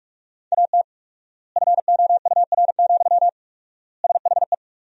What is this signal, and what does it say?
Morse code, heard as a keyed tone, about 46 wpm, 700 Hz. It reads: AT FOUR, SHE
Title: 4x Word-Spacing - Sentences from Top 300 Words - 45wpm